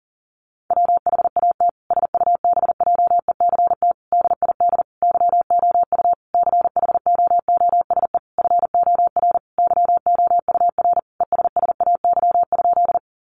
Morse code, heard as a keyed tone, about 40 wpm, 700 Hz. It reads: WHAT SUBJECT DID YOU CHOOSE FOR YOUR ESSAY?
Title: Sentences from Top 1500 Words 40wpm